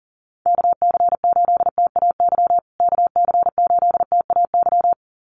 Morse code, heard as a keyed tone, about 40 wpm, 700 Hz. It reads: KC8TAY KC8TAY
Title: USA Call Signs Repeated 2x 40wpm